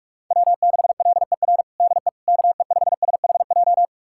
Morse code, heard as a keyed tone, about 46 wpm, 700 Hz. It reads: W6LER DE KE5SHJ
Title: USA Callsign DE Callsign 45wpm